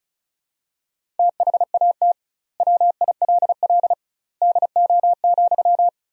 Morse code, heard as a keyed tone, about 36 wpm, 700 Hz. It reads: THAT WILL DO,
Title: 2x Word-Spacing - Sentences from Top 100 Words - 35wpm